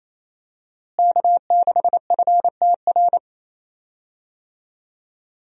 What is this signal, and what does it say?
Morse code, heard as a keyed tone, about 28 wpm, 700 Hz. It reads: K6FTR
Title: USA Call Signs - 5 Characters - Repeated 3x - 28wpm